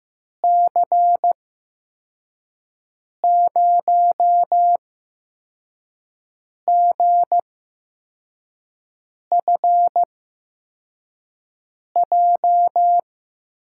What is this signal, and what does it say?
Morse code, heard as a keyed tone, about 15 wpm, spaced as for 4 wpm, 700 Hz. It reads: C0GFJ
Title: USA Call Signs - 6 Characters - 8x Character Spacing - 15wpm